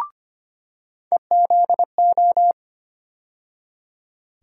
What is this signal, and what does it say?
Morse code, heard as a keyed tone, about 25 wpm, 700 Hz. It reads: EZO
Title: USA Call Sign Postfixes - 3 Letters 25wpm